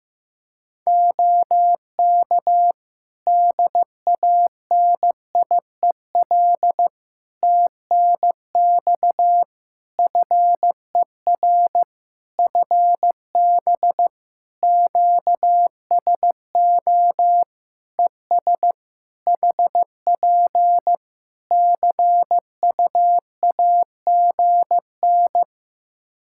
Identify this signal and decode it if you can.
Morse code, heard as a keyed tone, about 15 wpm, 700 Hz. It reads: OK DANIEL TNX FER FB QSO ES HP CUAGN
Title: Practice QSO - Exchange 4 - Ending - 15wpm